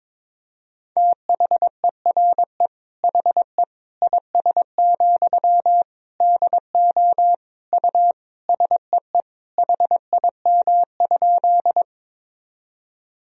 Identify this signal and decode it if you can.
Morse code, heard as a keyed tone, about 22 wpm, 700 Hz. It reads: THERE HE IS, DO U SEE HIM?
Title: Sentences from Top 100 Words and Abbreviations 22wpm